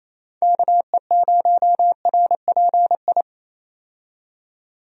Morse code, heard as a keyed tone, about 27 wpm, 700 Hz. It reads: KE0RPI
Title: USA Call Signs - 6 Characters - 28wpm